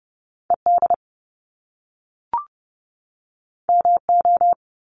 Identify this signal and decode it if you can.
Morse code, heard as a keyed tone, about 30 wpm, 700 Hz. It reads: ED MO
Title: Top Two Letter Combinations in Words 30wpm